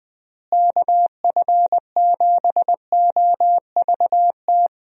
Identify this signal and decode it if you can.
Morse code, heard as a keyed tone, about 20 wpm, 700 Hz. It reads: KF7OVT
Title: USA Call Signs - 6 Characters - Repeated 2x - 20wpm